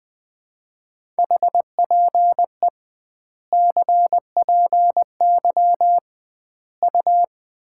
Morse code, heard as a keyed tone, about 19 wpm, 700 Hz. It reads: HPE CPY U
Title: Common QSO Phrases - 2x Word-Spacing - 20wpm